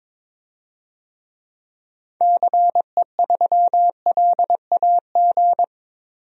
Morse code, heard as a keyed tone, about 21 wpm, 700 Hz. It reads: CE3LAG
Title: DXCC-100 Call Signs - 6 Characters - 22wpm